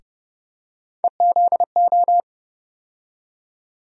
Morse code, heard as a keyed tone, about 30 wpm, 700 Hz. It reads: EZO
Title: USA Call Sign Postfixes - 3 Letters 30wpm